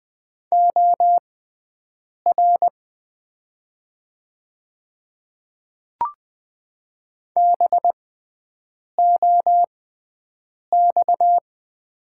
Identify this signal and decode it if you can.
Morse code, heard as a keyed tone, about 20 wpm, spaced as for 7 wpm, 700 Hz. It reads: OR BOX
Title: 6x Character Spacing - Top 500 Words - 20wpm